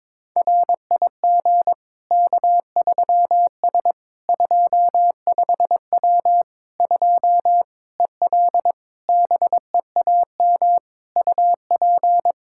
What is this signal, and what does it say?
Morse code, heard as a keyed tone, about 22 wpm, 700 Hz. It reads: RIG K3S 25W 2 EL BEAM UP
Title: Practice QSO - Exchange 2 - Info - 22wpm